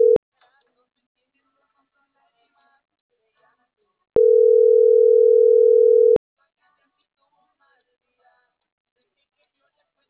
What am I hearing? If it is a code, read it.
Ringback tone (North America), 440 + 480 Hz.